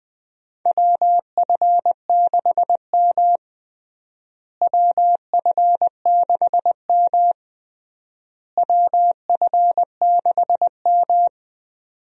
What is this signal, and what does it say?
Morse code, heard as a keyed tone, about 20 wpm, 700 Hz. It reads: WF6M WF6M WF6M
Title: USA Call Signs - 4 Characters - Repeated 3x - 20wpm